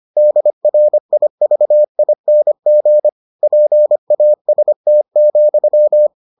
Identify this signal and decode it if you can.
Morse code, heard as a keyed tone, about 25 wpm, 600 Hz. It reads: DRIVING PAST,